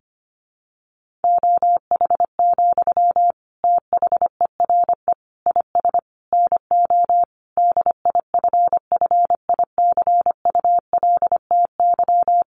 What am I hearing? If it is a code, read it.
Morse code, heard as a keyed tone, about 25 wpm, 700 Hz. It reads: OH, THERE IS NO DIFFICULTY
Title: Sentences from Top 1500 Words 25wpm